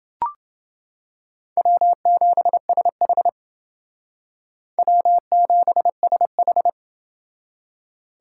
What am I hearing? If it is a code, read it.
Morse code, heard as a keyed tone, about 28 wpm, 700 Hz. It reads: W7SH W7SH